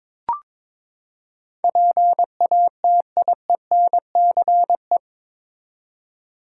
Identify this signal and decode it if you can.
Morse code, heard as a keyed tone, about 22 wpm, 700 Hz. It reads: PATIENCE
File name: practice / 22wpm / Eight Letter Words - 22wpm